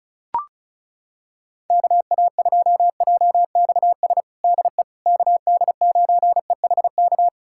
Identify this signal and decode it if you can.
Morse code, heard as a keyed tone, about 36 wpm, 700 Hz. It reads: KA2JXS DE KD9EHK
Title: USA Callsign DE Callsign - 6 Characters - 35wpm